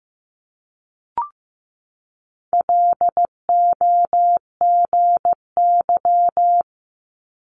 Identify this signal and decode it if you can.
Morse code, heard as a keyed tone, about 15 wpm, 700 Hz. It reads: LOGY
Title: Multi-letter Phonemes 15wpm